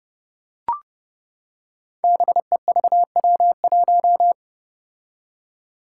Morse code, heard as a keyed tone, about 30 wpm, 700 Hz. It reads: BEVW1
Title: Five Challenging Letter-Number Combinations 30wpm